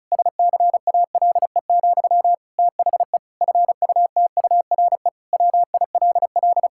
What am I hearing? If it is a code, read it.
Morse code, heard as a keyed tone, about 36 wpm, 700 Hz. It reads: SCALE, THE FUTURE WILL
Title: Sentences from Top 1000 Words - Review Part 5 - 35wpm